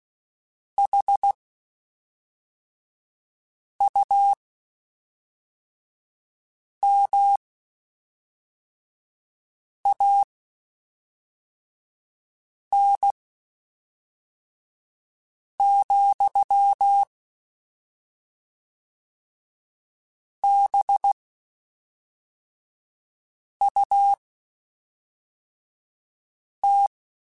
Morse code, heard as a keyed tone, about 16 wpm, spaced as for 3 wpm, 800 Hz. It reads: HUMAN,BUT
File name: Spaced / Chuckles Spaced-mp3s / HumanErr sp @5